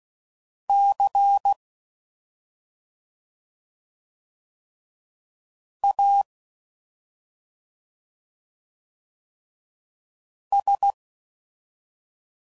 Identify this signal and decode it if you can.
Morse code, heard as a keyed tone, about 16 wpm, 800 Hz. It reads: C A S
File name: Suffixes / Special ING's / Sngl ING's